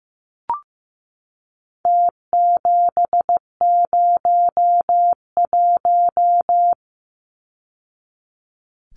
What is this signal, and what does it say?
Morse code, heard as a keyed tone, about 15 wpm, 700 Hz. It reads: T701
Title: Numbers - Contest Style - 4-Digit 15wpm